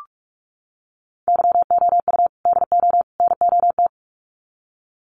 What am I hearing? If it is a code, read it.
Morse code, heard as a keyed tone, about 44 wpm, 700 Hz. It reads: YOU DO NOT
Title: Sets of 3 Words 45wpm